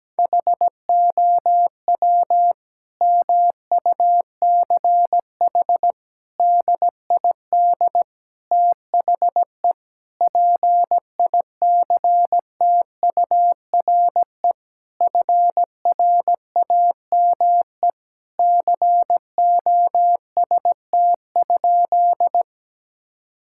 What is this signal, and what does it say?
Morse code, heard as a keyed tone, about 17 wpm, 700 Hz. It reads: HOW MUCH DID THE PICTURE FRAME COST?
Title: Sentences from Top 1000 Words - Review Part 6 - 17wpm